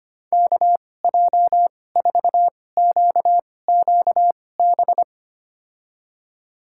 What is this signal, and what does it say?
Morse code, heard as a keyed tone, about 25 wpm, spaced as for 18 wpm, 700 Hz. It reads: KJ4QQB
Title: USA Call Signs - 6 Characters - 2x Character Spacing - 25wpm